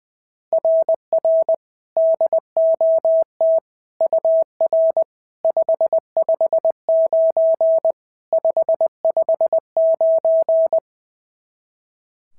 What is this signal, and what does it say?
Morse code, heard as a keyed tone, about 20 wpm, 660 Hz. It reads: RR DOT UR 559 559
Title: SOTA QSO Practice 20wpm